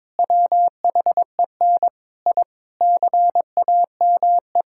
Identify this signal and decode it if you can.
Morse code, heard as a keyed tone, about 22 wpm, 700 Hz. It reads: WHEN I CAME
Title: Sentences from Top 1000 Words - Review Part 6 - 22wpm